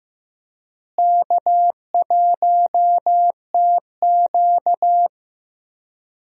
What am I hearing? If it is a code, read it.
Morse code, heard as a keyed tone, about 15 wpm, 700 Hz. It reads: K1TQ
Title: USA Call Signs - 4 Characters - Repeated 2x - 15wpm